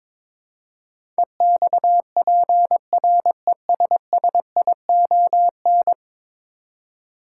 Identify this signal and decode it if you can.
Morse code, heard as a keyed tone, about 23 wpm, 700 Hz. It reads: EXPRESSION
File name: practice / 22wpm / 10 Letter Words - First 3 Letters Given - 22wpm